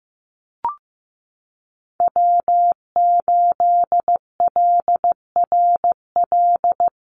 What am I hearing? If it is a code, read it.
Morse code, heard as a keyed tone, about 15 wpm, 700 Hz. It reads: W8LRL
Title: USA Call Signs 15wpm